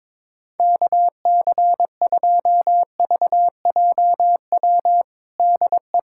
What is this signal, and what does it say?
Morse code, heard as a keyed tone, about 22 wpm, 700 Hz. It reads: KC2VJW DE
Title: USA Callsign DE Callsign 22wpm